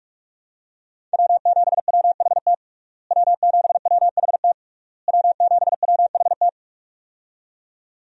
Morse code, heard as a keyed tone, about 46 wpm, 700 Hz. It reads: W7WHT W7WHT W7WHT
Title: USA Call Signs - 5 Characters - Repeated 3x - 45wpm